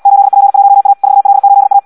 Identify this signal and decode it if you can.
Morse code, heard as a keyed tone, about 74 wpm, 800 Hz. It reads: 8P1A 4L8A
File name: Audio practica CW / qrq-tx-78 2